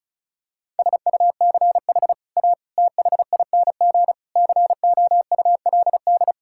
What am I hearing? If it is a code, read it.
Morse code, heard as a keyed tone, about 36 wpm, 700 Hz. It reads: SUCH A THING COULD
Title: Sentences from Top 1000 Words - Review Part 6 - 35wpm